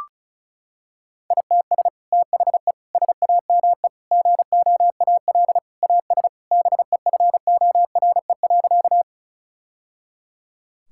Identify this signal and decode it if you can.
Morse code, heard as a keyed tone, about 36 wpm, 700 Hz. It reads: ITS THE SAME GOAL AS BEFORE.
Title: Sentences from Top 1000 Words - Review Part 5 - 35wpm